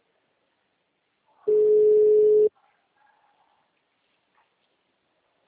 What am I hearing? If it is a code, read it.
Ringback tone (Europe, 425 Hz).